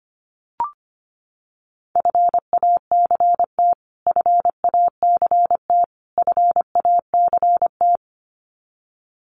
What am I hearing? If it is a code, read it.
Morse code, heard as a keyed tone, about 25 wpm, 700 Hz. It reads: FACT FACT FACT
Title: Top 500 Words Repeated 25wpm